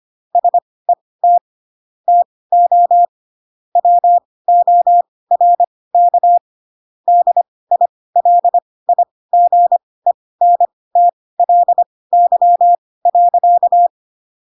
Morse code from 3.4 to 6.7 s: WORK